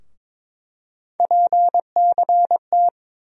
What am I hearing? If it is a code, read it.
Morse code, heard as a keyed tone, about 22 wpm, 700 Hz. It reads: PCT
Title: USA Call Sign Postfixes - 3 Letters 22wpm